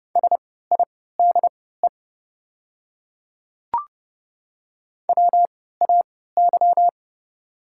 Morse code, heard as a keyed tone, about 30 wpm, spaced as for 17 wpm, 700 Hz. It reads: SIDE WAY
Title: 3x Character Spacing - Top 200 Words - 30wpm